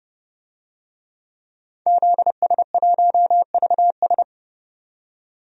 Morse code, heard as a keyed tone, about 30 wpm, 700 Hz. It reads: ZS1VS
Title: DXCC-100 Call Signs - 5 Characters - 30wpm